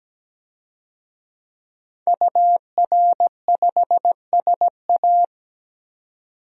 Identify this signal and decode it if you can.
Morse code, heard as a keyed tone, about 17 wpm, 700 Hz. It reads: UR5SA
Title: DXCC-100 Call Signs - 5 Characters - 17wpm